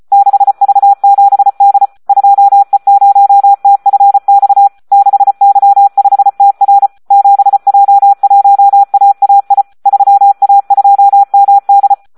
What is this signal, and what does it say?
Morse code, heard as a keyed tone, about 34 wpm, 800 Hz. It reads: 6V7D 2E0TFX 6Y5TR 7J1AAI 3A2MD
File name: Audio practica CW / SD 1 34